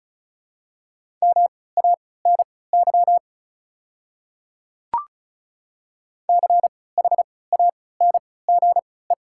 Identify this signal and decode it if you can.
Morse code, heard as a keyed tone, about 36 wpm, spaced as for 20 wpm, 700 Hz. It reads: MANY CHANGE
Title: 3x Character Spacing - Top 300 Words - 35wpm